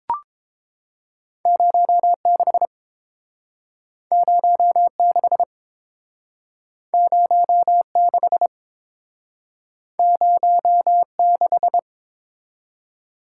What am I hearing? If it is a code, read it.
Morse code, heard as a keyed tone, about 30 wpm, 700 Hz. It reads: 06 06 06 06